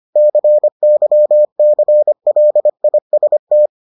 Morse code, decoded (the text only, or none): CYCLIST